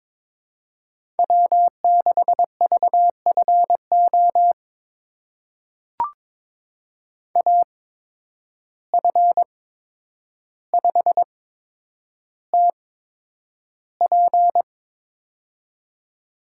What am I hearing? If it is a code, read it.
Morse code, heard as a keyed tone, about 22 wpm, spaced as for 9 wpm, 700 Hz. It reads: W6VFO AF5TP
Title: USA Call Signs - 5 Characters - 8x Character Spacing - 22wpm